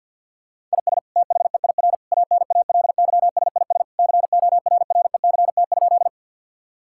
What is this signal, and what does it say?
Morse code, heard as a keyed tone, about 47 wpm, 700 Hz. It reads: IS THEIR ANALYSIS CORRECT?
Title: Sentences from Top 1000 Words - Review Part 5 - 50wpm